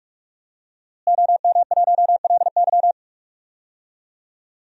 Morse code, heard as a keyed tone, about 46 wpm, 700 Hz. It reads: OM1LY
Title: DXCC-100 Call Signs - 5 Characters - Repeated 3x - 45wpm